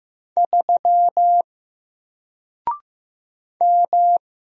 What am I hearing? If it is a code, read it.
Morse code, heard as a keyed tone, about 15 wpm, 700 Hz. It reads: 3 M
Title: Single Letter-Number Alt 15wpm